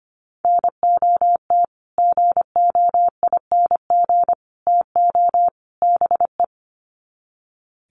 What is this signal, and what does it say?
Morse code, heard as a keyed tone, about 25 wpm, 700 Hz. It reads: NOT GOING TO BE
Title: Sets of 4 Words 25wpm